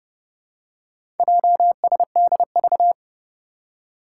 Morse code, heard as a keyed tone, about 31 wpm, 700 Hz. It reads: JSDV
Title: Four Challenging Letter-Number Combinations 30wpm